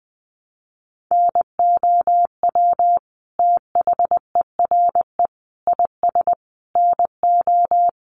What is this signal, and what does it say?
Morse code, heard as a keyed tone, about 20 wpm, 700 Hz. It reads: NOW THERE IS NO
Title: Sentences from Top 100 Words 20wpm